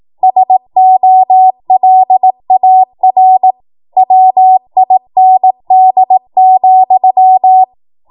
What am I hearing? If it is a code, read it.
Morse code, heard as a keyed tone, about 18 wpm, 750 Hz. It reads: SOLAR WIND,
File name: Morse / 18WPM